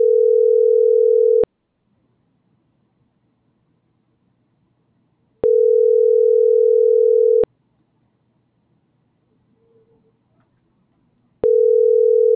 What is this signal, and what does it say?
Ringback tone (North America), 440 + 480 Hz.